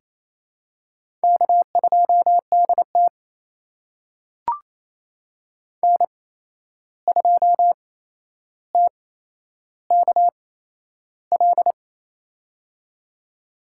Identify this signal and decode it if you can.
Morse code, heard as a keyed tone, about 28 wpm, spaced as for 11 wpm, 700 Hz. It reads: K2DT N2TKL
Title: USA Call Signs - 8x Character Spacing - 28wpm